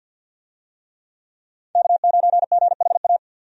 Morse code, heard as a keyed tone, about 52 wpm, 700 Hz. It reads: K9GHA
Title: USA Call Signs - 5 Characters - 50wpm